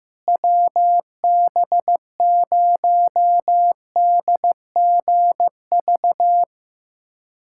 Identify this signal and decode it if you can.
Morse code, heard as a keyed tone, about 15 wpm, 700 Hz. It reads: WB0DGV
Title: USA Callsign DE Callsign 15wpm